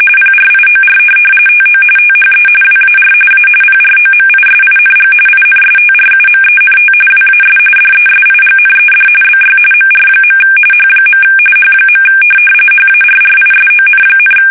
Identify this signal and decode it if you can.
RTTY, about 45.45 baud. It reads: PRIOR TO THE WAR A PROMISING INSTRUMENT WAS DESIGNED BY MR. H. H. HARRISON,